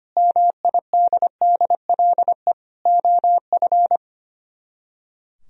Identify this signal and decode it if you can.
Morse code, heard as a keyed tone, about 31 wpm, 700 Hz. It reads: MIDDLE OF
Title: Speed-Racing - Sets of 4 Words 25wpm